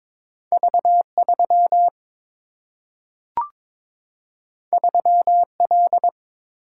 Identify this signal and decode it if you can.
Morse code, heard as a keyed tone, about 22 wpm, 700 Hz. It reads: V3 3L
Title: Single-Letter Single-Number 22wpm